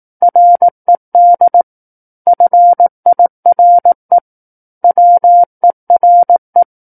Morse code, heard as a keyed tone, about 18 wpm, 700 Hz. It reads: RED FIRE WERE
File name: 18 WPM